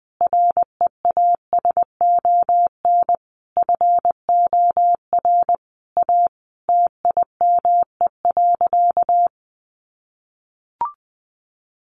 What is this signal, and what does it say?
Morse code, heard as a keyed tone, about 20 wpm, 700 Hz. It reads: REASON FOR A TIME.
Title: Sentences from Top 700 Words 20wpm